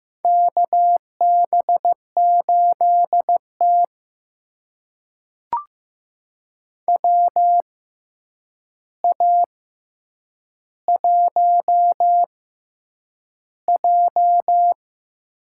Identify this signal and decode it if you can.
Morse code, heard as a keyed tone, about 15 wpm, spaced as for 8 wpm, 700 Hz. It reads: KB8T WA1J
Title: USA Call Signs - 4 Characters - 6x Character Spacing - 15wpm